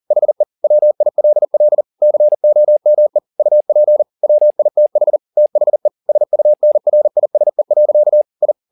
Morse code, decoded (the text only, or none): HE WILL COME UP WITH THE SUNRISE. I